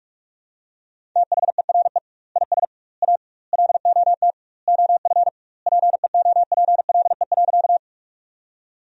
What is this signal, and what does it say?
Morse code, heard as a keyed tone, about 46 wpm, 700 Hz. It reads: THERE IS A LOT OF PEOPLE.
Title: 2x Word-Spacing - Sentences from Top 500 Words - 45wpm